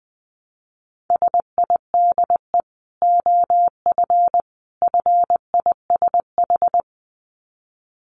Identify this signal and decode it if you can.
Morse code, heard as a keyed tone, about 20 wpm, 700 Hz. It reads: SIDE OF FISH
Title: Dit Challenge 20wpm